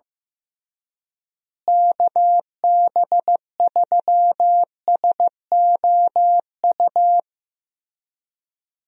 Morse code, heard as a keyed tone, about 14 wpm, 700 Hz. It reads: KB3SOU
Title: USA Call Signs - 6 Characters - Repeated 3x - 15wpm